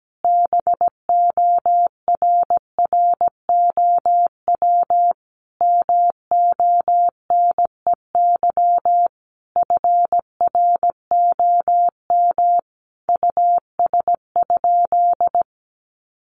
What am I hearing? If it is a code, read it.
Morse code, heard as a keyed tone, about 17 wpm, 700 Hz. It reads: BORROW MONEY FROM US?
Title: Sentences from Top 1500 Words 17wpm